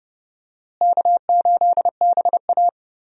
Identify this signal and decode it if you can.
Morse code, heard as a keyed tone, about 30 wpm, 700 Hz. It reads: K8BA
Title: USA Call Signs - 4 Characters - Repeated 3x - 30wpm